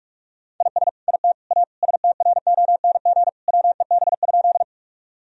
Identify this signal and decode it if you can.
Morse code, heard as a keyed tone, about 42 wpm, 700 Hz. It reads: IS IT A STRONG WEB?
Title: Sentences from Top 1000 Words - Review Part 4 - 45wpm